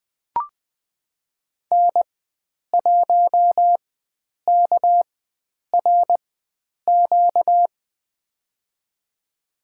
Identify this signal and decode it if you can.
Morse code, heard as a keyed tone, about 20 wpm, spaced as for 9 wpm, 700 Hz. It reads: N1KRQ
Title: USA Call Signs - 4x Character Spacing - 20wpm